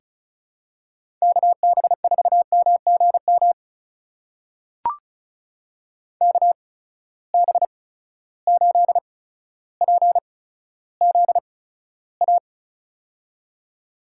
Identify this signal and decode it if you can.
Morse code, heard as a keyed tone, about 36 wpm, spaced as for 15 wpm, 700 Hz. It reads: KB4MGM KB8PZA
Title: USA Call Signs - 6 Characters - 8x Character Spacing - 35wpm